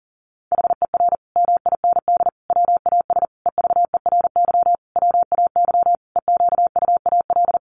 Morse code, heard as a keyed tone, about 40 wpm, 700 Hz. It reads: HER MIND WAS EVERY WAY EQUAL
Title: Sentences from Top 1400 Words 40wpm